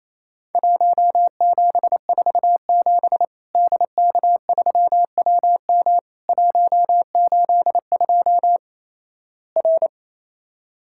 Morse code, heard as a keyed tone, about 28 wpm, 700 Hz. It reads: 1747 DK3WM 182 R
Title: Worked All Europe DX Contest - QTC Exchange - 28wpm